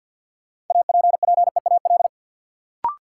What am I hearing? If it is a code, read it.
Morse code, heard as a keyed tone, about 53 wpm, 700 Hz. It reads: APPEAL
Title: Sentences from Top 1300 Words - New Word Drill Alt - 50wpm